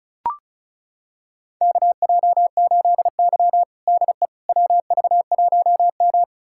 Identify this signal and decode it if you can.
Morse code, heard as a keyed tone, about 34 wpm, 700 Hz. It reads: KJ8Y DE WV1M
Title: USA Callsign DE Callsign - 4 Characters - 35wpm